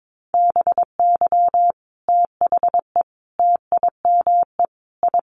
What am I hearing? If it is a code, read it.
Morse code, heard as a keyed tone, about 22 wpm, 700 Hz. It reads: BY THE TIME I
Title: Sets of 5 Words Encore2 22wpm